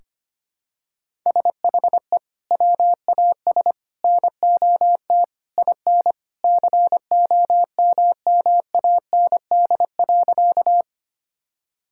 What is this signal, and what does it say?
Morse code, heard as a keyed tone, about 25 wpm, 700 Hz. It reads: SHE WAS NOT IN COMMAND.
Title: Sentences from Top 1600 Words 25wpm